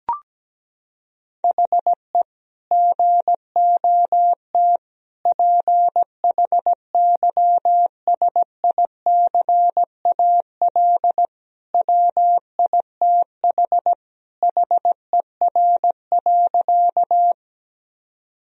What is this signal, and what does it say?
Morse code, heard as a keyed tone, about 17 wpm, 700 Hz. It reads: HE GOT PHYSICAL WITH HER.
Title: Sentences from Top 1000 Words - Review Part 4 - 17wpm